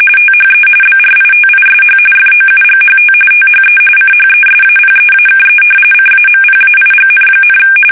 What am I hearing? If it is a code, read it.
RTTY, about 45.45 baud. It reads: THE BRITISH FOUGHT THE CHESAPEAKE CAMPAIGN,
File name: fsk850 / usnavy2